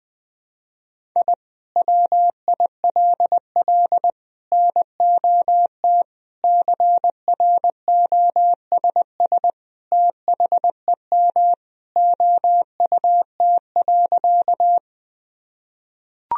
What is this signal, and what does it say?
Morse code, heard as a keyed tone, about 21 wpm, 700 Hz. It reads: I WILL NOT CROSS THEM OUT.
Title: Sentences from Top 1000 Words - Review Part 1 - 20wpm